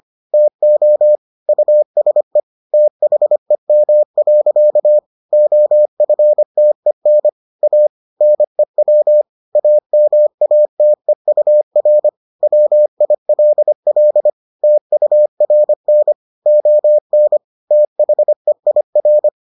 Morse code, heard as a keyed tone, about 25 wpm, 600 Hz. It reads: TO USE THEM. OFTEN A NEW AMATEUR WILL TURN ON THEIR